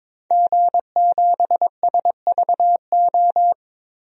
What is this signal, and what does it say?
Morse code, heard as a keyed tone, about 22 wpm, 700 Hz. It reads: G7SVO
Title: DXCC-100 Call Signs - 5 Characters - Repeated 3x - 22wpm